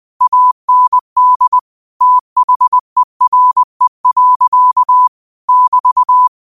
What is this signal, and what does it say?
Morse code, heard as a keyed tone, about 20 wpm, 1000 Hz. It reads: AND THERE. <BT>